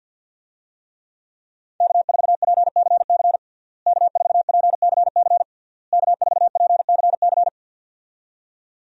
Morse code, heard as a keyed tone, about 52 wpm, 700 Hz. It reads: K4PCC K4PCC K4PCC
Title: USA Call Signs - 5 Characters - Repeated 3x - 50wpm